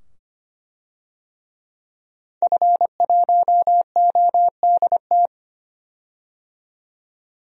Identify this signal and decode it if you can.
Morse code, heard as a keyed tone, about 25 wpm, 700 Hz. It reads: F1ODT
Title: DXCC-100 Call Signs - 5 Characters - 25wpm